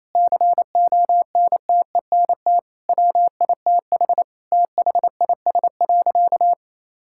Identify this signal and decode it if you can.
Morse code, heard as a keyed tone, about 28 wpm, 700 Hz. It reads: CONTENT WITH THIS.
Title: Sentences from Top 1000 Words - Review Part 6 - 28wpm